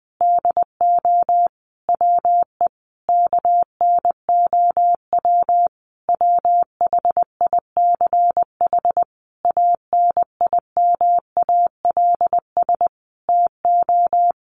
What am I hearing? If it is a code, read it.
Morse code, heard as a keyed tone, about 20 wpm, 700 Hz. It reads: DO WE KNOW WHICH ANIMALS TO
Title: A Sound Of Thunder 20wpm